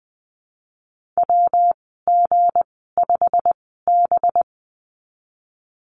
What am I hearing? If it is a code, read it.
Morse code, heard as a keyed tone, about 20 wpm, spaced as for 15 wpm, 700 Hz. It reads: WG5B
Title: 200 Lesson 32g - Letter k - Call Signs 20wpm